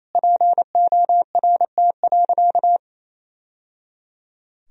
Morse code, heard as a keyed tone, about 28 wpm, 700 Hz. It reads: PORT.
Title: Sentences from Top 1000 Words - Review Part 1 - 28wpm